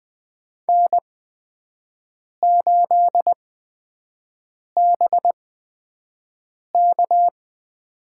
Morse code, heard as a keyed tone, about 21 wpm, spaced as for 5 wpm, 700 Hz. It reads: N8BK
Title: USA Call Signs - 4 Characters - 8x Character Spacing - 20wpm